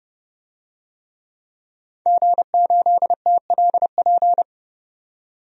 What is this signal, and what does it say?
Morse code, heard as a keyed tone, about 30 wpm, 700 Hz. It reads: G8TLP
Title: DXCC-100 Call Signs - 5 Characters - 30wpm